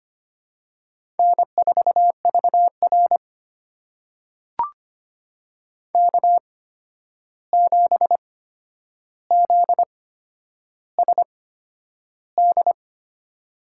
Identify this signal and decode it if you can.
Morse code, heard as a keyed tone, about 25 wpm, spaced as for 10 wpm, 700 Hz. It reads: N4VR K7ZSD
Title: USA Call Signs - 8x Character Spacing - 25wpm